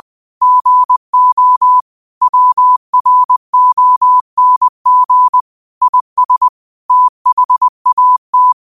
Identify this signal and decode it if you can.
Morse code, heard as a keyed tone, about 20 wpm, 1000 Hz. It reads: GO WRONG IS THAT
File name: qotdfiles / 20wpm